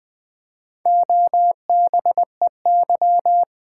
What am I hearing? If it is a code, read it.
Morse code, heard as a keyed tone, about 20 wpm, 700 Hz. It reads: OBEY
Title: Build A Word - 2 Letters At A Time - 4-Letter Words - 20wpm